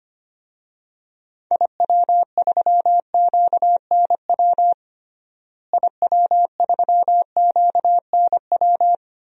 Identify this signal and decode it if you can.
Morse code, heard as a keyed tone, about 25 wpm, 700 Hz. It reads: IW3QNW IW3QNW
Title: DXCC-100 Call Signs - 6 Characters - Repeated 3x - 25wpm